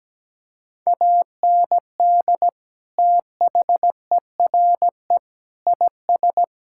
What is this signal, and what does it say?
Morse code, heard as a keyed tone, about 17 wpm, 700 Hz. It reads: AND THERE IS
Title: Sentences from Top 1600 Words 17wpm